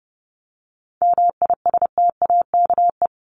Morse code, heard as a keyed tone, about 30 wpm, 700 Hz. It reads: MISTAKE
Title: Top 1501-1750 Words 30wpm